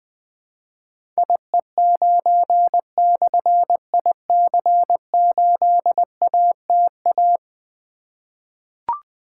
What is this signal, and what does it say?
Morse code, heard as a keyed tone, about 20 wpm, 700 Hz. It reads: IE9/IC8ATA
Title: Difficult Call Signs 20wpm